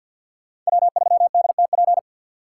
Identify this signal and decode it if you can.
Morse code, heard as a keyed tone, about 50 wpm, 700 Hz. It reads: W3DTP
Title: USA Call Signs - 5 Characters - 6x Character Spacing - 50wpm